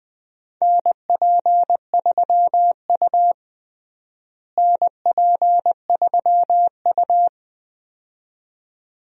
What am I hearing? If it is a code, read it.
Morse code, heard as a keyed tone, about 20 wpm, 700 Hz. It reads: NP3U NP3U
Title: USA Call Signs - 4 Characters - Repeated 2x - 20wpm